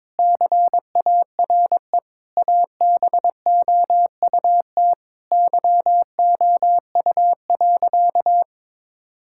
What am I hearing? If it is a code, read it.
Morse code, heard as a keyed tone, about 22 wpm, 700 Hz. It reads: CARE ABOUT YOU.
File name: Sentences with 7-Word Phrases - 22wpm